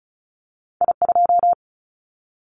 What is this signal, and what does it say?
Morse code, heard as a keyed tone, about 36 wpm, 700 Hz. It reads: I2
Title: Callsign Prefixes for Top 50 DXCC - 35wpm